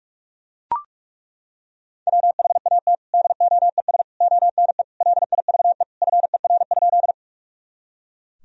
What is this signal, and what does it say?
Morse code, heard as a keyed tone, about 41 wpm, 700 Hz. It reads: WHAT DOES ONE LIVE FER?
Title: Sentences from Top 200 Words and Abbreviations 45wpm